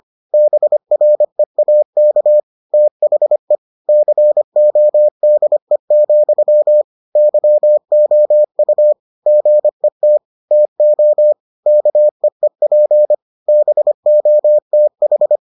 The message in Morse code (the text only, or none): BREAK THE CODE, YOU GET TO KEEP BOTH